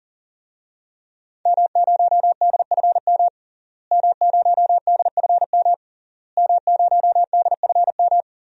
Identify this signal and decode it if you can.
Morse code, heard as a keyed tone, about 37 wpm, 700 Hz. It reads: M0DFM M0DFM M0DFM
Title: DXCC-100 Call Signs - 5 Characters - Repeated 3x - 40wpm